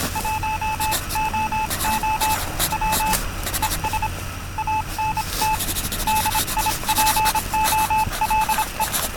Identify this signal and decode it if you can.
Morse code, heard as a keyed tone, about 26 wpm, 870 Hz, with noise. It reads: 1OO W ES ANT DIPOLE